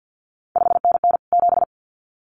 Morse code, heard as a keyed tone, about 50 wpm, 700 Hz. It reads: 5NN 7
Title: CQ WW DX Contest Exchange - 50wpm